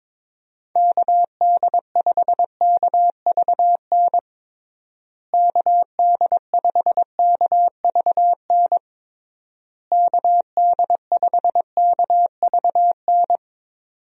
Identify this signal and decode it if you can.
Morse code, heard as a keyed tone, about 22 wpm, 700 Hz. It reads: KD5KVN KD5KVN KD5KVN
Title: USA Call Signs - 6 Characters - Repeated 3x - 22wpm